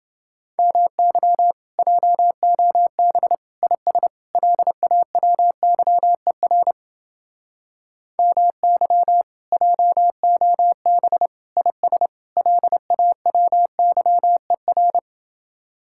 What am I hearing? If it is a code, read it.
Morse code, heard as a keyed tone, about 29 wpm, 700 Hz. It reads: MY JOB IS LAWYER MY JOB IS LAWYER